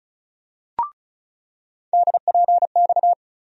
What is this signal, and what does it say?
Morse code, heard as a keyed tone, about 34 wpm, 700 Hz. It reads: DPX
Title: USA Call Sign Postfixes - 3 Letters 35wpm